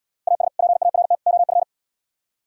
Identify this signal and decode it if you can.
Morse code, heard as a keyed tone, about 76 wpm, 700 Hz. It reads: AS PART OF